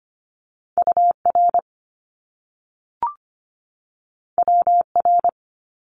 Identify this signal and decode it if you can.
Morse code, heard as a keyed tone, about 25 wpm, 700 Hz. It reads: UR WR
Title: Top Two Letter Combinations in Words 25wpm